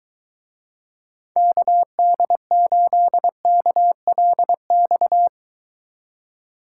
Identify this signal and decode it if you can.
Morse code, heard as a keyed tone, about 23 wpm, 700 Hz. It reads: KD8KLX